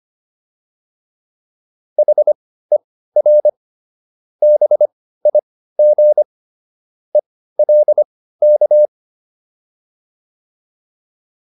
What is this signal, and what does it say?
Morse code, heard as a keyed tone, about 25 wpm, spaced as for 15 wpm, 600 Hz. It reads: HER BIG ELK